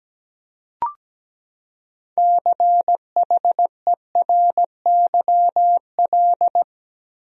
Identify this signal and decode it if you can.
Morse code, heard as a keyed tone, about 17 wpm, 700 Hz. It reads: CHERYL
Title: USA Common Names - 17wpm